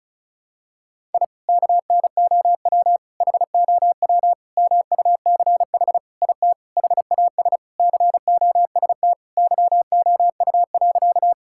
Morse code, heard as a keyed tone, about 34 wpm, 700 Hz. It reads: I KNOW HOW MUCH IT HAS COST YOU.
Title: Sentences from Top 1000 Words - Review Part 5 - 35wpm